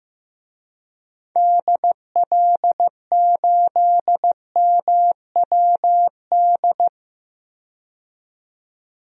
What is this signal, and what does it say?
Morse code, heard as a keyed tone, about 16 wpm, 700 Hz. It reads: DL8MWD
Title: DXCC-100 Call Signs - 6 Characters - 15wpm